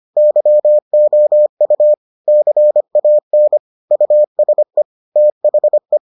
Morse code, heard as a keyed tone, about 25 wpm, 600 Hz. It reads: YOU CAN USE THE